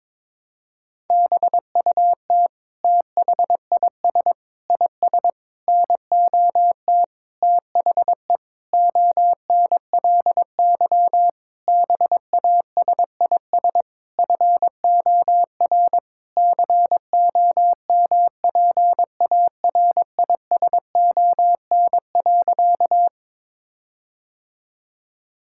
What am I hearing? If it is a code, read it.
Morse code, heard as a keyed tone, about 22 wpm, 700 Hz. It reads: BUT THIS IS NOT THE ONLY BASIS FOR COMPARISON.
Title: Sentences from Top 1600 Words 22wpm